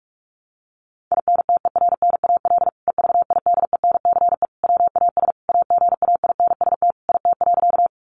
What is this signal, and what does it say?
Morse code, heard as a keyed tone, about 44 wpm, 700 Hz. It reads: INTERNAL EVIDENCE WAS AGAINST IT.
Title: Sentences from Top 1400 Words 45wpm